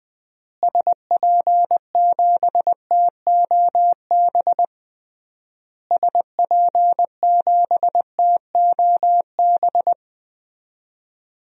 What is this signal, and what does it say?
Morse code, heard as a keyed tone, about 21 wpm, 700 Hz. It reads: SP7TOB SP7TOB
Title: DXCC-100 Call Signs - 6 Characters - Repeated 3x - 20wpm